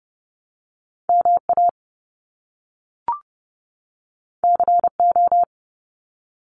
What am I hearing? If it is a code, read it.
Morse code, heard as a keyed tone, about 30 wpm, 700 Hz. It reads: MA CO
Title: US States 30wpm